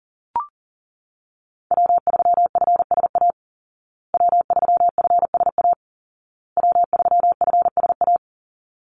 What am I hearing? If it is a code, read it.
Morse code, heard as a keyed tone, about 40 wpm, 700 Hz. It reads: W3FSA W3FSA W3FSA
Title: Call Signs Repeated 2x - 40wpm